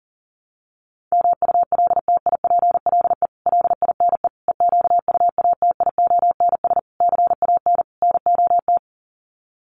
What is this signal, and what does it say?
Morse code, heard as a keyed tone, about 40 wpm, 700 Hz. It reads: MULTIPLE LINE EQUATIONS CAN NOT
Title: Sets of 5 Words 40wpm